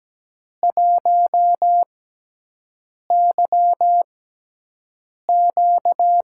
Morse code, heard as a keyed tone, about 17 wpm, spaced as for 6 wpm, 700 Hz. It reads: 1YQ
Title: USA Call Signs - 4 Characters - 6x Character Spacing - 17wpm